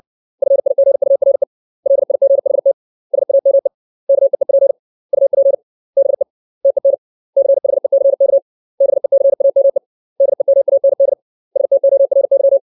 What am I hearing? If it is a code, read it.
Morse code, heard as a keyed tone, about 60 wpm, 550 Hz. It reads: FIGURE LIGHT SAME KEEP UP BE TEN CHECK BECAME DEMAND STORY
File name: 632 words / 632 Words 60WPM